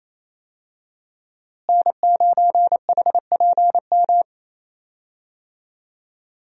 Morse code, heard as a keyed tone, about 27 wpm, 700 Hz. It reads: N9HPM
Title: USA Call Signs - 5 Characters - Repeated 3x - 28wpm